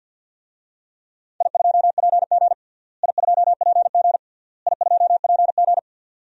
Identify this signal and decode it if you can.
Morse code, heard as a keyed tone, about 49 wpm, 700 Hz. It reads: I2PG I2PG I2PG
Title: DXCC-100 Call Signs - 4 Characters - Repeated 3x - 50wpm